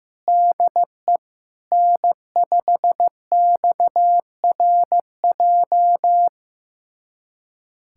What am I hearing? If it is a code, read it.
Morse code, heard as a keyed tone, about 15 wpm, 700 Hz. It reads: DE N5XRJ
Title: USA Callsign DE Callsign - 5 Characters - 15wpm